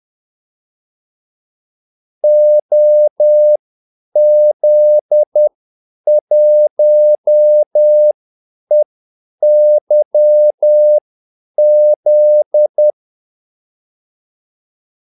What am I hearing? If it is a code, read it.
Morse code, heard as a keyed tone, about 10 wpm, spaced as for 8 wpm, 600 Hz. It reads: OZ1EYZ
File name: Callsigns-10-8wpm